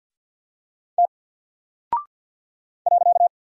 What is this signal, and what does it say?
Morse code, heard as a keyed tone, about 48 wpm, 700 Hz. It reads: T .
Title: Single Character (Letter:Number:Punctuation:Prosign) 50wpm